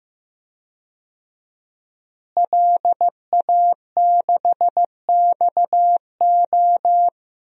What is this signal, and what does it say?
Morse code, heard as a keyed tone, about 15 wpm, 700 Hz. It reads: LA6XO
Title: DXCC-100 Call Signs - 5 Characters - Repeated 3x - 15wpm